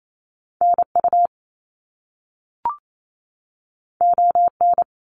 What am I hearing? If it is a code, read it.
Morse code, heard as a keyed tone, about 28 wpm, 700 Hz. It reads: NU ON
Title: Canadian Provinces and Territories 28wpm